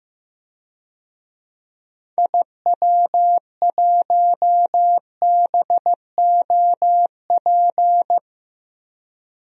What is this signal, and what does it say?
Morse code, heard as a keyed tone, about 15 wpm, 700 Hz. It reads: IW1BOP
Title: DXCC-100 Call Signs - 6 Characters - Repeated 2x - 15wpm